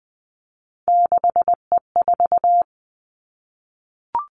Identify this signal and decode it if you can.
Morse code, heard as a keyed tone, about 20 wpm, 700 Hz. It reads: 6E4
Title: Three Challenging Letter-Number Combinations 20wpm